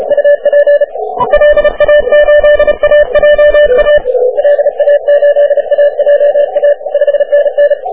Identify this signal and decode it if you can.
Morse code, heard as a keyed tone, about 34 wpm, 580 Hz. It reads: UP LA8AJA LA8AJA 5NN
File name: wp / 4k0cw 40m cw